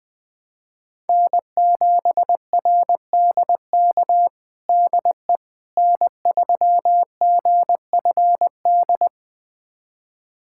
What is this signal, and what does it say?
Morse code, heard as a keyed tone, about 20 wpm, 700 Hz. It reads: N7RDK DE N3GFD
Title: USA Callsign DE Callsign - 5 Characters - 20wpm